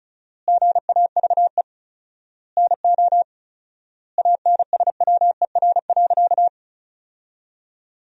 Morse code, heard as a keyed tone, about 36 wpm, 700 Hz. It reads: GAVE NO ANSWER.
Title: 4x Word-Spacing - Sentences from Top 500 Words - 35wpm